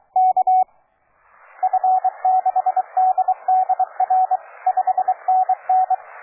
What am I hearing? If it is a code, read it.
Morse code, heard as a keyed tone, about 21 wpm, 750 Hz. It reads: K F6DDR 5NN
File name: cw forum / OJ0B sur 1823 le 21-09-2008 17h00 test GP XXL